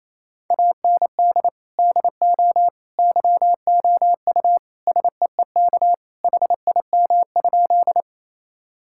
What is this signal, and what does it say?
Morse code, heard as a keyed tone, about 28 wpm, 700 Hz. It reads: AND DO YOU SEEK HIM?
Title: Sentences from Top 1600 Words 28wpm